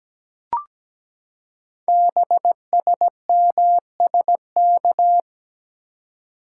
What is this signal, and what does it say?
Morse code, heard as a keyed tone, about 17 wpm, 700 Hz. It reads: BSMSK
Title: Code Groups - 5 Letters - 17wpm